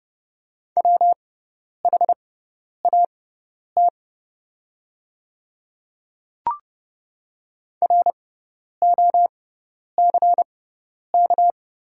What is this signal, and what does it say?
Morse code, heard as a keyed tone, about 29 wpm, spaced as for 10 wpm, 690 Hz. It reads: WHAT ROCK
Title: 6x Character Spacing - Top 500 Words - 30wpm